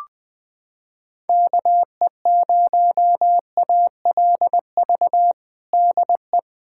Morse code, heard as a keyed tone, about 20 wpm, 700 Hz. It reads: KE0ALV DE
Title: USA Callsign DE Callsign - 6 Characters - 20wpm